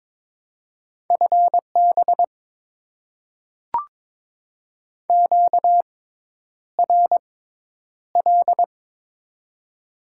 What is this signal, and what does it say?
Morse code, heard as a keyed tone, about 22 wpm, spaced as for 10 wpm, 700 Hz. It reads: FB QRL
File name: Top 90 QSO Elements - 6x Character Spacing - 22wpm